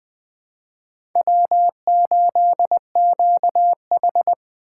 Morse code, heard as a keyed tone, about 20 wpm, 700 Hz. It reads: W8QH
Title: USA Call Signs - 4 Characters - 3x Character Spacing - 20wpm